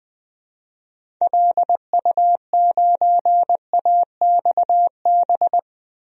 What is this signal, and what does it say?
Morse code, heard as a keyed tone, about 21 wpm, 700 Hz. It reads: LU9AXB
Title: DXCC-100 Call Signs - 6 Characters - Repeated 2x - 20wpm